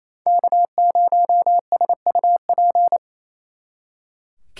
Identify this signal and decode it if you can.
Morse code, heard as a keyed tone, about 35 wpm, 700 Hz. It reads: K0SUP